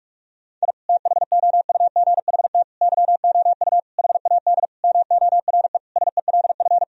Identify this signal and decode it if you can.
Morse code, heard as a keyed tone, about 47 wpm, 700 Hz. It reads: I THOUGHT YOU HAD MORE SELF
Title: Sentences from Top 1000 Words - Review Part 3 - 45wpm